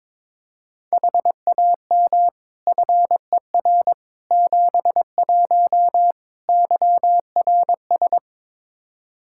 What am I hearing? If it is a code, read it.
Morse code, heard as a keyed tone, about 22 wpm, 700 Hz. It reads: HAM FER 71 YRS
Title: K1USN SST QSO Practice 22wpm